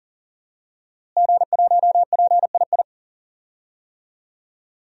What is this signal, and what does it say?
Morse code, heard as a keyed tone, about 40 wpm, 700 Hz. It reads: G1PII
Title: DXCC-100 Call Signs - 5 Characters - Repeated 3x - 40wpm